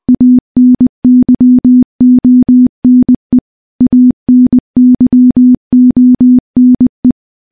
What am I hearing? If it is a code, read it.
Morse code, heard as a keyed tone, about 21 wpm, 250 Hz. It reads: ANYONE ANYONE
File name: morse-training-20WPM